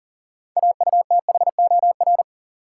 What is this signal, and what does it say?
Morse code, heard as a keyed tone, about 40 wpm, 700 Hz. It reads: AUTHOR
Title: Sentences from Top 1200 Words - New Word Drill Alt - 40wpm